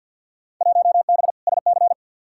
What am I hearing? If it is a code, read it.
Morse code, heard as a keyed tone, about 48 wpm, 700 Hz. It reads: 1B SC
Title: Field Day Exchanges - 50wpm